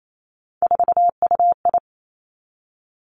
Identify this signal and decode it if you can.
Morse code, heard as a keyed tone, about 28 wpm, 700 Hz. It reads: 4UI
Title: Three Challenging Letter-Number Combinations 28wpm